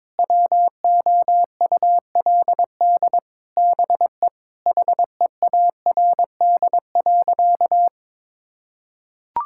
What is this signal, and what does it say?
Morse code, heard as a keyed tone, about 22 wpm, 700 Hz. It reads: WOULD BE HEARD.
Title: Sentences from Top 1600 Words 22wpm